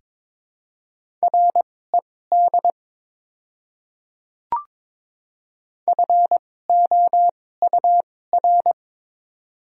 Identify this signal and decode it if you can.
Morse code, heard as a keyed tone, about 22 wpm, spaced as for 16 wpm, 700 Hz. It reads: RED FOUR
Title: 2x Character Spacing - Top 500 Words - 22wpm